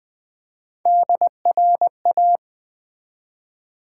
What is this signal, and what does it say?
Morse code, heard as a keyed tone, about 20 wpm, 700 Hz. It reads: DRA
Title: Build A Word - 1 Letter At A Time - 5-Letter Words - 20wpm